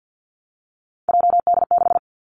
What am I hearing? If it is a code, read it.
Morse code, heard as a keyed tone, about 50 wpm, 700 Hz. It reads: JD6
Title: Three Challenging Letter-Number Combinations 50wpm